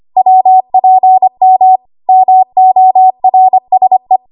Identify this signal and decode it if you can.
Morse code, heard as a keyed tone, about 25 wpm, 750 Hz. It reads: WPM MORSE
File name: Archive / 25 WPM / 25WPM